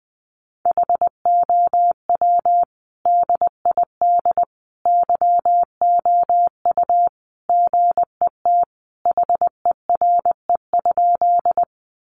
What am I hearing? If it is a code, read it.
Morse code, heard as a keyed tone, about 20 wpm, 700 Hz. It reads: HOW DID YOU GET HERE?
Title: Sentences from Top 200 Words 20wpm